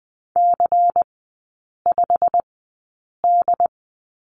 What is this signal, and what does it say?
Morse code, heard as a keyed tone, about 20 wpm, spaced as for 8 wpm, 690 Hz. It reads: C5D
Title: 159 Lesson 26e - Letter g - Three Characters 20wpm